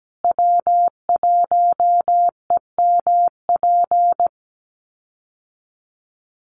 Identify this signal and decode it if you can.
Morse code, heard as a keyed tone, about 17 wpm, 700 Hz. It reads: W1EMP
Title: USA Call Signs Repeated 3x 17wpm